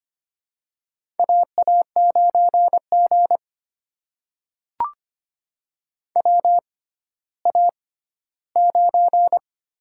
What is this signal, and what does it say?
Morse code, heard as a keyed tone, about 25 wpm, spaced as for 14 wpm, 700 Hz. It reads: AA9G WA9
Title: USA Call Signs - 4 Characters - 6x Character Spacing - 25wpm